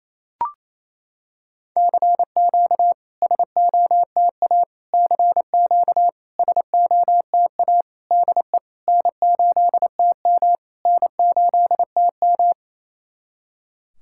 Morse code, heard as a keyed tone, about 28 wpm, 690 Hz. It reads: CQ SOTA CQ SOTA DE N8TM N8TM
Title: SOTA QSO Practice 28wpm